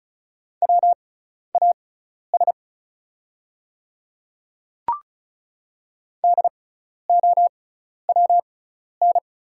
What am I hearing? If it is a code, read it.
Morse code, heard as a keyed tone, about 36 wpm, spaced as for 12 wpm, 700 Hz. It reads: WAS DOWN
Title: 6x Character Spacing - Top 100 Words - 35wpm